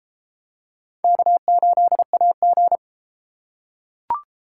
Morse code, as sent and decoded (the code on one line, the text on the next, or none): -.- ---.. .- --.
K8AG